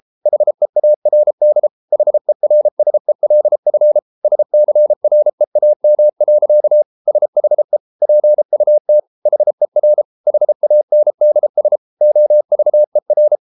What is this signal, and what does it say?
Morse code, heard as a keyed tone, about 33 wpm, 600 Hz. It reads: HEARD HERSELF SCREAM. SHE PUT HER HANDS OVER